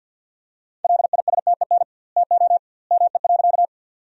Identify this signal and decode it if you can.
Morse code, heard as a keyed tone, about 50 wpm, 700 Hz. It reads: LISTEN TO ME.
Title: 2x Word-Spacing - Sentences from Top 400 Words - 50wpm